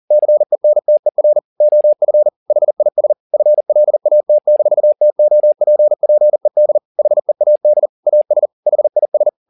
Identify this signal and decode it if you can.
Morse code, heard as a keyed tone, about 40 wpm, 600 Hz. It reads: CENTER OF HIS FLAT-TOPPED HEAD AS HIS
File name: the old man and the sea 2 40wpm